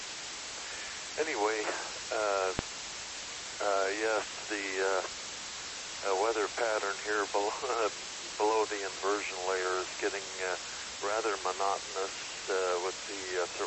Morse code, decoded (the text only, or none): none